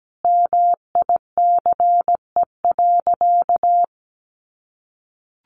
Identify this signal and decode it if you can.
Morse code, heard as a keyed tone, about 17 wpm, 700 Hz. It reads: MICE.
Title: A Sound Of Thunder 17wpm